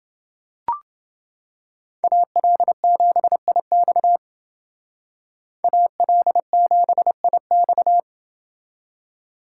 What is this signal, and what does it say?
Morse code, heard as a keyed tone, about 29 wpm, 700 Hz. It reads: AL7IX AL7IX